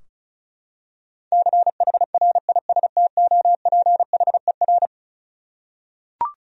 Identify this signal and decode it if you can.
Morse code, heard as a keyed tone, about 34 wpm, 700 Hz. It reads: CHRISTOPHER
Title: USA Common Names - 35wpm